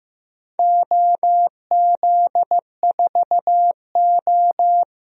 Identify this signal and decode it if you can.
Morse code, heard as a keyed tone, about 15 wpm, 700 Hz. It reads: OZ4O
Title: DXCC-100 Call Signs - 4 Characters - Repeated 2x - 15wpm